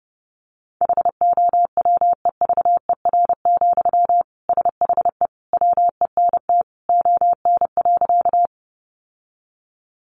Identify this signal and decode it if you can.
Morse code, heard as a keyed tone, about 30 wpm, 700 Hz. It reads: HOWEVER, SHE WENT ON.
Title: Sentences from Top 600 Words 30wpm